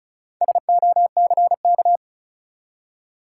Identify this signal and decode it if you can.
Morse code, heard as a keyed tone, about 36 wpm, 700 Hz. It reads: SOCK
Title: Four Letter Words - 35wpm